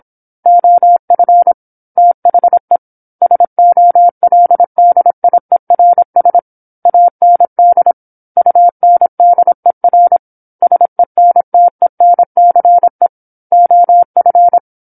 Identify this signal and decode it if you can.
Morse code, heard as a keyed tone, about 26 wpm, 700 Hz. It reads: OF THE SOLDIERS AND UNDER SENTENCE OF
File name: 08-Chapter-08 26 WPM